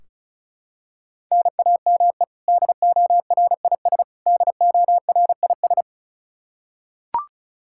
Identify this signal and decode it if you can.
Morse code, heard as a keyed tone, about 36 wpm, 700 Hz. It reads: NAME DORIS DORIS
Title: Practice QSO - Exchange 1 - Beginning - 35wpm